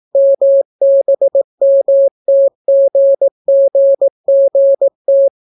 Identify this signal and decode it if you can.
Morse code, heard as a keyed tone, about 18 wpm, 550 Hz. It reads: MBMTGGGT